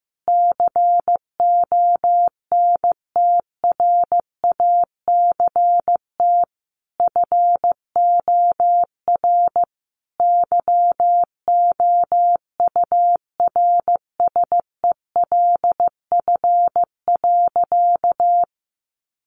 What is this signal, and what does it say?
Morse code, heard as a keyed tone, about 14 wpm, 700 Hz. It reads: CONTRACT FOR YOURSELF.
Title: Sentences from Top 1400 Words 15wpm